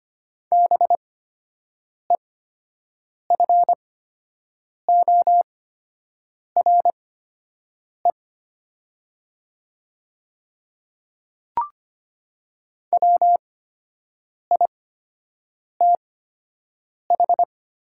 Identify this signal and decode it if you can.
Morse code, heard as a keyed tone, about 25 wpm, spaced as for 7 wpm, 700 Hz. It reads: BEFORE WITH
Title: 8x Character Spacing - Top 200 Words - 25wpm